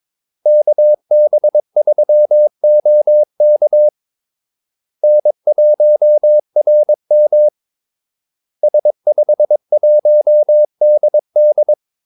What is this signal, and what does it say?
Morse code, heard as a keyed tone, about 22 wpm, 600 Hz. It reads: KB3OK N1RM S51DD
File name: Callsigns-22wpm